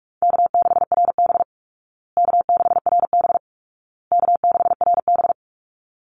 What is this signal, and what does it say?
Morse code, heard as a keyed tone, about 44 wpm, 700 Hz. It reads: K6RB K6RB K6RB
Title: Easy Call Signs Repeated 3x - 45wpm